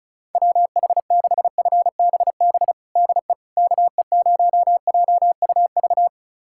Morse code, heard as a keyed tone, about 36 wpm, 700 Hz. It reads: WH6FBB DE KE0JUV
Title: USA Callsign DE Callsign 35wpm